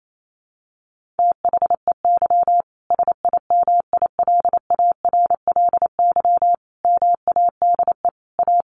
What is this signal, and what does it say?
Morse code, heard as a keyed tone, about 28 wpm, 700 Hz. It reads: THEY SIMILARLY MADE A
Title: Sentences from Top 1400 Words 28wpm